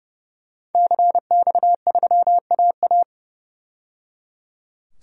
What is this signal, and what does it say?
Morse code, heard as a keyed tone, about 30 wpm, 700 Hz. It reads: CX3AA
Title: DXCC-100 Call Signs - 5 Characters - 30wpm